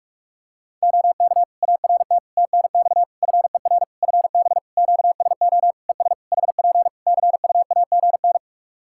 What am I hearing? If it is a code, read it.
Morse code, heard as a keyed tone, about 46 wpm, 700 Hz. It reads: OK ART TNX FER FB QSO ES HP CUAGN
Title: Practice QSO - Exchange 4 - Ending - 45wpm